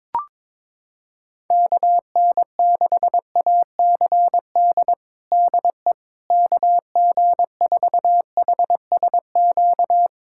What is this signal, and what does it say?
Morse code, heard as a keyed tone, about 22 wpm, 700 Hz. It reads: KN6ACD DE KG4HSQ
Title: USA Callsign DE Callsign 22wpm